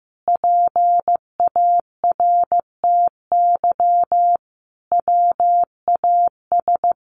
Morse code, heard as a keyed tone, about 15 wpm, 700 Hz. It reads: PARTY WAS
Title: Sentences from Top 1400 Words 15wpm